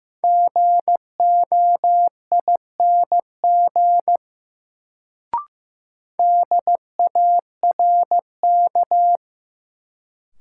Morse code, heard as a keyed tone, about 15 wpm, 700 Hz. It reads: GOING DARK
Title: Sentences from Top 500 Words - New Word Drill - 15wpm